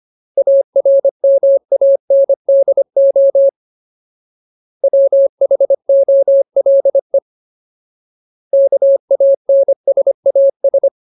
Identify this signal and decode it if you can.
Morse code, heard as a keyed tone, about 25 wpm, 550 Hz. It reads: ARMANDO WHOLE KANSAS
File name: Ultimate Word List / 25WPM Ultimate